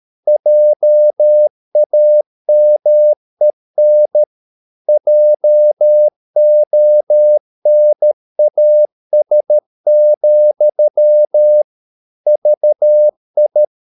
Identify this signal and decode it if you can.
Morse code, heard as a keyed tone, about 13 wpm, 600 Hz. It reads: JAMEN JONAS, VI
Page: Ferie 13 wpm